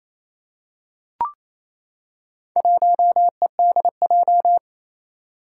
Morse code, heard as a keyed tone, about 27 wpm, 700 Hz. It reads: 1EDJ
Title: Four Challenging Letter-Number Combinations 28wpm